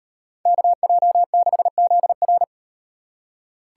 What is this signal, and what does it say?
Morse code, heard as a keyed tone, about 38 wpm, 700 Hz. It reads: KJ6ZR